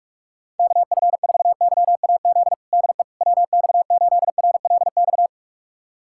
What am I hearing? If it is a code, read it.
Morse code, heard as a keyed tone, about 41 wpm, 700 Hz. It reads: KF4YAZ DE WX8RLX
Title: USA Callsign DE Callsign 45wpm